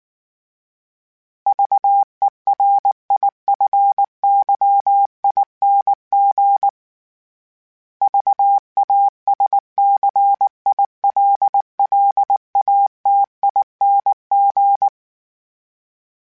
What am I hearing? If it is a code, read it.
Morse code, heard as a keyed tone, about 19 wpm, 800 Hz. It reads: VERIFYING VASCILLATING
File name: Special INGs / Special INGs-mp3s / Sngl Ing @18